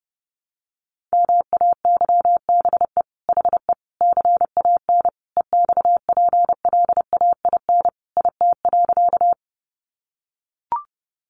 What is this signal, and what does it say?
Morse code, heard as a keyed tone, about 30 wpm, 700 Hz. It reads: MAYBE HE CAN EXPLAIN IT.
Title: Sentences from Top 1100 Words 30wpm